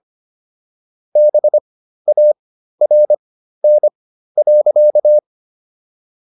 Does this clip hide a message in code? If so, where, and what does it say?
Yes: Morse code from 1.2 to 5.2 s, heard as a keyed tone, about 25 wpm, spaced as for 13 wpm, 600 Hz. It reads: BARN.